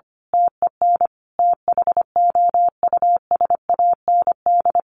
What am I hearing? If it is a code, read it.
Morse code, heard as a keyed tone, about 25 wpm, 700 Hz. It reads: TEN THOUSAND
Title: A Sound Of Thunder 25wpm